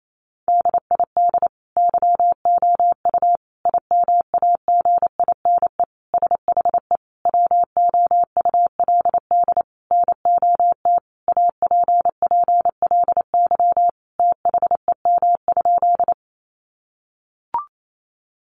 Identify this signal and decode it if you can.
Morse code, heard as a keyed tone, about 28 wpm, 700 Hz. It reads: DID YOU IMAGINE SHE WOULD NOT APPLY THEM?
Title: Sentences from Top 1400 Words 28wpm